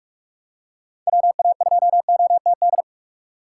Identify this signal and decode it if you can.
Morse code, heard as a keyed tone, about 45 wpm, 700 Hz. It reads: WA2OTD
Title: USA Call Signs - 6 Characters - 4x Character Spacing - 45wpm